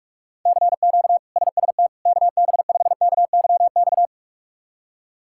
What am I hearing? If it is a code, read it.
Morse code, heard as a keyed tone, about 46 wpm, 700 Hz. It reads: CQ SST KB5KYX
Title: K1USN SST QSO Practice - 45wpm